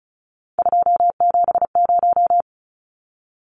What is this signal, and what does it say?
Morse code, heard as a keyed tone, about 34 wpm, 700 Hz. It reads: 270
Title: Numbers - 3-Digit 35wpm